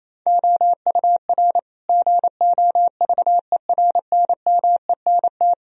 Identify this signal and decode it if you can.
Morse code, heard as a keyed tone, about 28 wpm, 690 Hz. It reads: OUR GOVERNMENT
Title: Sentences from Top 1000 Words - Review Part 3 - 28wpm